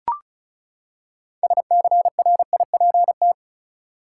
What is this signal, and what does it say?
Morse code, heard as a keyed tone, about 36 wpm, 700 Hz. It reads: SCRIPT
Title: Six Letter Words - 35wpm